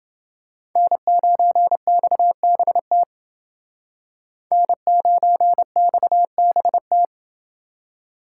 Morse code, heard as a keyed tone, about 28 wpm, 700 Hz. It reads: N9XBT N9XBT